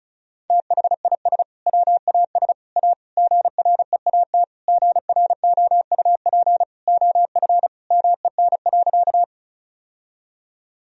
Morse code, heard as a keyed tone, about 34 wpm, 700 Hz. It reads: THIS WAS A GREAT GROUP OF MEN.
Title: Sentences from Top 1000 Words - Review Part 1 - 35wpm